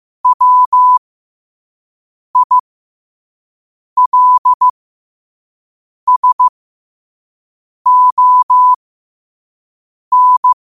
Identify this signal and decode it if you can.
Morse code, heard as a keyed tone, about 15 wpm, spaced as for 5 wpm, 1000 Hz. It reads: WILSON